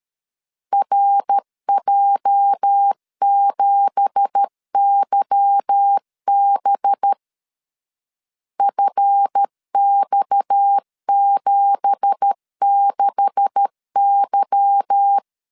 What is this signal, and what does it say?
Morse code, heard as a keyed tone, about 13 wpm, 800 Hz. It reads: RJ7YB FX76Y
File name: LEZIONI / 9 2867 12wpm 2322